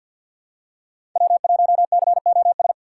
Morse code, heard as a keyed tone, about 53 wpm, 700 Hz. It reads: W1COS
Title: USA Call Signs - 5 Characters - 2x Character Spacing - 50wpm